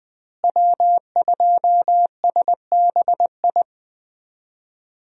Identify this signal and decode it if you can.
Morse code, heard as a keyed tone, about 20 wpm, 700 Hz. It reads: W2SBI
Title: USA Call Signs - 5 Characters - Repeated 3x - 20wpm